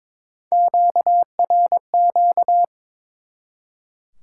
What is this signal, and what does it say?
Morse code, heard as a keyed tone, about 22 wpm, 700 Hz. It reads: QRQ
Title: Top 90 QSO Elements 22wpm